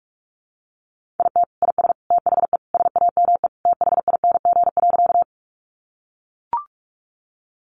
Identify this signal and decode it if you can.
Morse code, heard as a keyed tone, about 44 wpm, 700 Hz. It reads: IT IS THE SAME THING.
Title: Sentences from Top 200 Words 45wpm